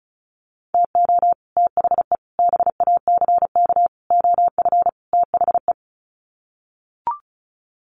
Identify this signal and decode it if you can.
Morse code, heard as a keyed tone, about 34 wpm, 700 Hz. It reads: TO THE BACK OF THE
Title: Sets of 5 Words 35wpm